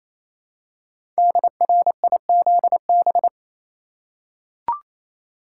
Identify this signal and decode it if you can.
Morse code, heard as a keyed tone, about 28 wpm, 700 Hz. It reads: DRIZB
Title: Code Groups - 5 Letters - 28wpm